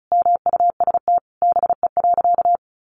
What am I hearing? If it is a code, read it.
Morse code, heard as a keyed tone, about 34 wpm, 700 Hz. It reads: MUST BE.
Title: Sentences from Top 1400 Words 35wpm